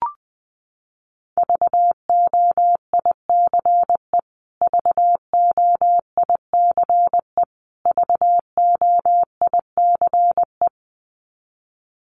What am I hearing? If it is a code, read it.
Morse code, heard as a keyed tone, about 20 wpm, 700 Hz. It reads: VOICE VOICE VOICE
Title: Top 500 Words Repeated 20wpm